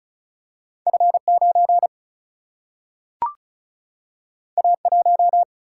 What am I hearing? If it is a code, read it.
Morse code, heard as a keyed tone, about 36 wpm, 700 Hz. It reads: F9 A1
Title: Single-Letter Single-Number 35wpm